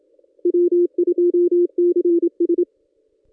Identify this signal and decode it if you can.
Morse code, heard as a keyed tone, about 27 wpm, 350 Hz. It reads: W2CS
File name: contests / ss-cw-k5tr-op-n5rz -1900